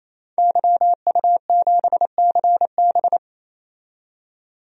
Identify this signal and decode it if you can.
Morse code, heard as a keyed tone, about 27 wpm, 700 Hz. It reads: YU7CB
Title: DXCC-100 Call Signs - 5 Characters - 28wpm